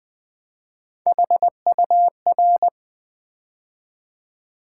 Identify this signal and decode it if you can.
Morse code, heard as a keyed tone, about 21 wpm, 700 Hz. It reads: HUR
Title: Build A Word - 3 Letters At A Time - 9-Letter Words - 20wpm